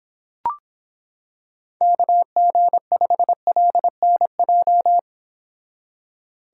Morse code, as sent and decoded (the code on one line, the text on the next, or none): -.- --. ..... .-.. -. .---
KG5LNJ